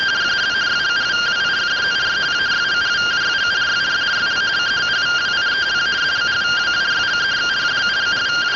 Two-tone teleprinter signal, about 170 Hz shift.